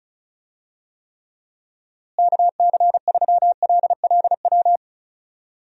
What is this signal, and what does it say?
Morse code, heard as a keyed tone, about 36 wpm, 700 Hz. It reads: KC3LLW
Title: USA Call Signs - 6 Characters - Repeated 3x - 35wpm